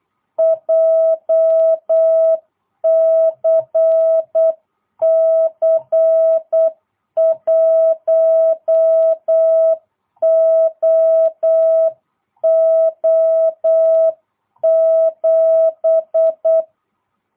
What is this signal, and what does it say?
Morse code, heard as a keyed tone, about 8 wpm, 660 Hz. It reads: JCC 1OO7